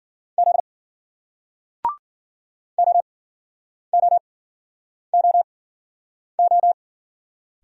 Morse code, heard as a keyed tone, about 53 wpm, spaced as for 9 wpm, 700 Hz. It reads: Z OOOO